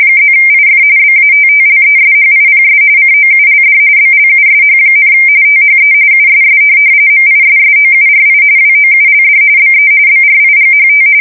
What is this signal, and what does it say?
RTTY, about 45.45 baud. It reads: +IF BATTLESHIPS COULD BLUSH, THE USS MISSOURI WOULD BE BRIGHT
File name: fsk170 / USS-Missouri